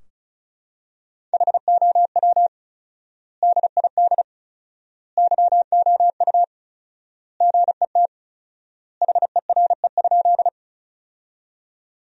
Morse code, heard as a keyed tone, about 36 wpm, 700 Hz. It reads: HOW DID YOU GET HERE?
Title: 4x Word-Spacing - Sentences from Top 200 Words - 35wpm